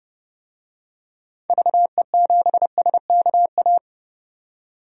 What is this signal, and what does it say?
Morse code, heard as a keyed tone, about 30 wpm, 700 Hz. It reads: VE7SKA
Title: DXCC-100 Call Signs - 6 Characters - 30wpm